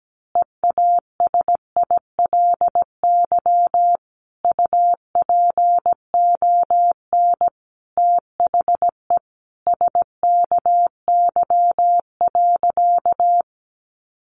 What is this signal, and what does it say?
Morse code, heard as a keyed tone, about 17 wpm, 700 Hz. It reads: EASILY UPON THE SKY.
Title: A Sound Of Thunder 17wpm